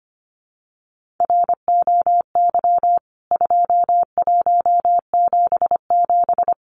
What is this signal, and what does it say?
Morse code, heard as a keyed tone, about 25 wpm, 700 Hz. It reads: ROY 2177
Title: CWT Exchange 25wpm